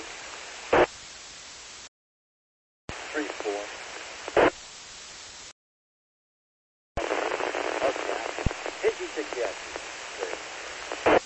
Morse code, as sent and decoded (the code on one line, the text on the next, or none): none